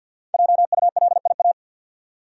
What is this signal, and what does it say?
Morse code, heard as a keyed tone, about 53 wpm, 700 Hz. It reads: JULIA
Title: USA Common Names - 50wpm